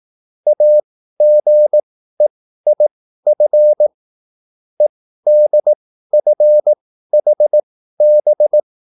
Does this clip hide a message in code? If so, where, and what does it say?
Yes: Morse code from 0.5 to 8.6 s, heard as a keyed tone, about 18 wpm, spaced as for 13 wpm, 600 Hz. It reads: AGEIF EDFHB